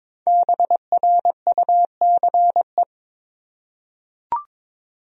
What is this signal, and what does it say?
Morse code, heard as a keyed tone, about 22 wpm, 700 Hz. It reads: BRUCE
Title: USA Common Names - Max 5 Letters - 22wpm